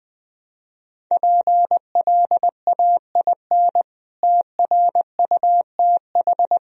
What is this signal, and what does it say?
Morse code, heard as a keyed tone, about 20 wpm, 700 Hz. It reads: PLAIN TRUTH
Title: Sentences from Top 1000 Words - Review Part 2 - 20wpm